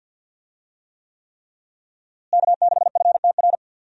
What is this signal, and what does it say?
Morse code, heard as a keyed tone, about 48 wpm, 700 Hz. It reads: K6FTR
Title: USA Call Signs - 5 Characters - Repeated 3x - 50wpm